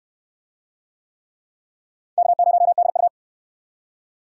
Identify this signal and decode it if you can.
Morse code, heard as a keyed tone, about 65 wpm, 700 Hz. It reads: K0DU